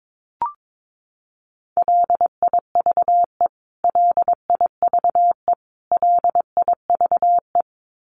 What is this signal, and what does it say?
Morse code, heard as a keyed tone, about 22 wpm, 700 Hz. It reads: LIVE LIVE LIVE
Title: Top 500 Words Repeated 22wpm